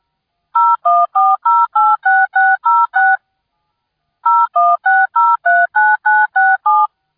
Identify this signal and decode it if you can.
DTMF, heard as telephone touch tones, tones of about 220 ms. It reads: 014086606 016039967